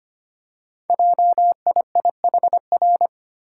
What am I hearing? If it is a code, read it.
Morse code, heard as a keyed tone, about 25 wpm, 700 Hz. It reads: JIIHR
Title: Code Groups - 5 Letters - 25wpm